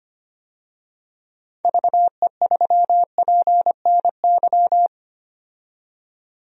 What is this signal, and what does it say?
Morse code, heard as a keyed tone, about 25 wpm, 700 Hz. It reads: VE3PNY
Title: DXCC-100 Call Signs - 6 Characters - 25wpm